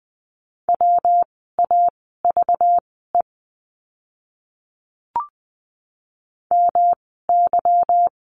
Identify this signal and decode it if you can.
Morse code, heard as a keyed tone, about 20 wpm, spaced as for 15 wpm, 700 Hz. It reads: WAVE MY
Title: 192 Lesson 31f - Letter v - Words 20wpm